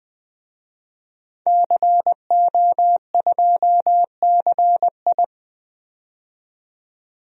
Morse code, heard as a keyed tone, about 20 wpm, 700 Hz. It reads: CO2CI
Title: DXCC-100 Call Signs - 5 Characters - 20wpm